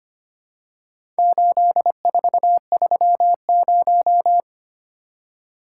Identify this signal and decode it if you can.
Morse code, heard as a keyed tone, about 25 wpm, 700 Hz. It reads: 8430
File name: practice / 25wpm / Four Letter-Number Combinations - 25wpm